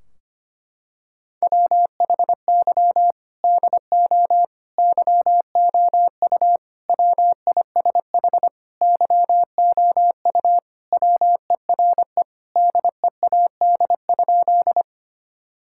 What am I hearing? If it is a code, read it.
Morse code, heard as a keyed tone, about 25 wpm, 700 Hz. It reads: WHY DO YOU WISH YOU WERE DEAD?
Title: Sentences from Top 1600 Words 25wpm